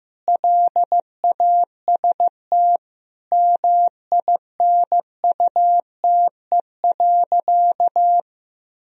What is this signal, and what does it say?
Morse code, heard as a keyed tone, about 16 wpm, 700 Hz. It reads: LAST MINUTE.
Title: Sentences from Top 1000 Words - Review Part 2 - 15wpm